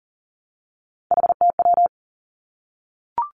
Morse code, heard as a keyed tone, about 38 wpm, 700 Hz. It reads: HTW
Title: Three Challenging Letter-Number Combinations 40wpm